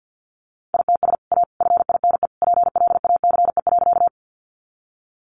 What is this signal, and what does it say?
Morse code, heard as a keyed tone, about 50 wpm, 700 Hz. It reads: ITS A FINE PLACE.
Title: Sentences from Top 600 Words 50wpm